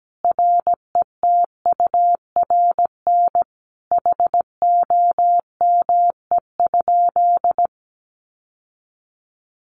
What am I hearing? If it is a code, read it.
Morse code, heard as a keyed tone, about 17 wpm, 700 Hz. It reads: RETURN HOME?
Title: Sentences from Top 800 Words 17wpm